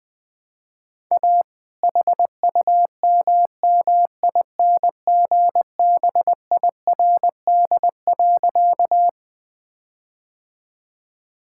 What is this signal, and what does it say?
Morse code, heard as a keyed tone, about 20 wpm, 700 Hz. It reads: A HUMMINGBIRD.
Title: All-Age Appropriate Jokes - 20wpm